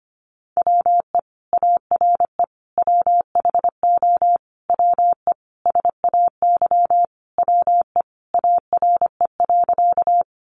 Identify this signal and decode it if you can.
Morse code, heard as a keyed tone, about 25 wpm, 700 Hz. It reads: WE ARE WHO WE SAY WE ARE.
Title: Sentences from Top 200 Words 25wpm